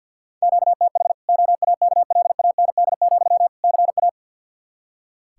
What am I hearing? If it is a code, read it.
Morse code, heard as a keyed tone, about 45 wpm, 700 Hz. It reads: QTH OAKLAND, CA
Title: K1USN SST QSO Practice 50wpm